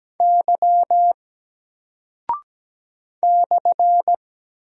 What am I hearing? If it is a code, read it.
Morse code, heard as a keyed tone, about 17 wpm, 700 Hz. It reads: Y /
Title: Single Character (Letter:Number:Punctuation:Prosign) Alt 17wpm